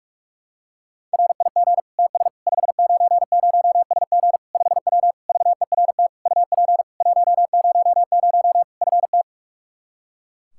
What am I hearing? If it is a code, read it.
Morse code, heard as a keyed tone, about 41 wpm, 700 Hz. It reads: RIG TS 590SG 5W VERT UP 100 FT
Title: Practice QSO - Exchange 2 - Info - 45wpm